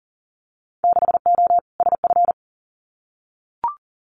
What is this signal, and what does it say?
Morse code, heard as a keyed tone, about 37 wpm, 700 Hz. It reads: 6O SF
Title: Winter Field Day Exchanges Encore 40wpm